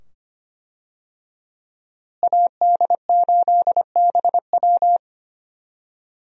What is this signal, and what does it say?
Morse code, heard as a keyed tone, about 25 wpm, 700 Hz. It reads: AD8BW
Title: USA Call Signs - 5 Characters - Repeated 2x - 25wpm